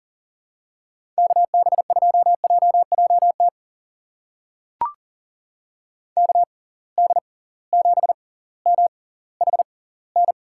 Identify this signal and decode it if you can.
Morse code, heard as a keyed tone, about 40 wpm, spaced as for 20 wpm, 700 Hz. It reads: KB2JJT KD7MHN
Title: USA Call Signs - 6 Characters - 6x Character Spacing - 40wpm